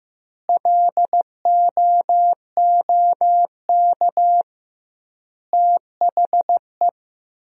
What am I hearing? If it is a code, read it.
Morse code, heard as a keyed tone, about 16 wpm, 700 Hz. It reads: LOOK THE
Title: 2x Word-Spacing - Sentences from Top 100 Words - 15wpm